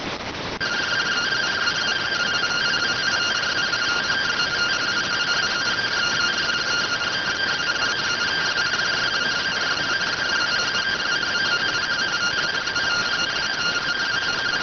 Two-tone teleprinter signal, about 170 Hz shift.